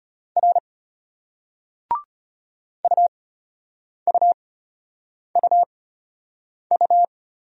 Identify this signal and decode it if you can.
Morse code, heard as a keyed tone, about 34 wpm, spaced as for 8 wpm, 700 Hz. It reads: R UUUU